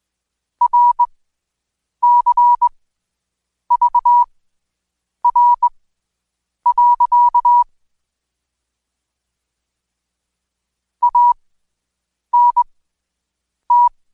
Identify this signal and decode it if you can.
Morse code, heard as a keyed tone, about 21 wpm, spaced as for 7 wpm, 970 Hz. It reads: RCVR. ANT
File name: Code Quick / AudioDownld2 / Seg113-Day18-DayDrill 18